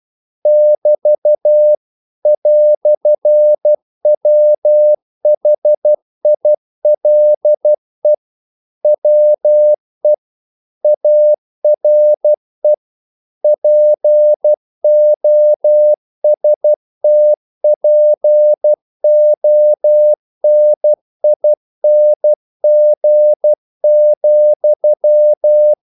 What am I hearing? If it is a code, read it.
Morse code, heard as a keyed tone, about 12 wpm, 600 Hz. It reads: <BT> "WHILE WE ARE POSTPONING,
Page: English quotations in CW (telegraphy - morse)